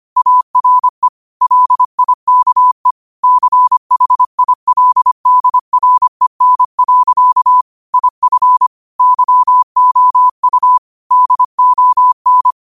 Morse code, heard as a keyed tone, about 25 wpm, 1000 Hz. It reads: ARE LIKE CHILDREN. IF YOU DON
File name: qotdfiles / 25wpm